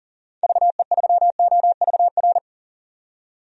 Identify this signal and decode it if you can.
Morse code, heard as a keyed tone, about 40 wpm, 700 Hz. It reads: VE3OVR
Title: DXCC-100 Call Signs - 6 Characters - 40wpm